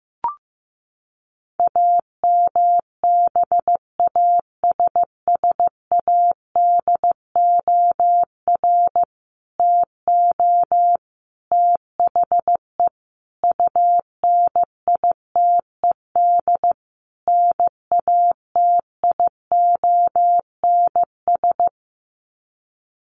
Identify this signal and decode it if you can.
Morse code, heard as a keyed tone, about 15 wpm, 700 Hz. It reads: AMBASSADOR TO THE UNITED NATIONS
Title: Sets of 5 Words Encore2 15wpm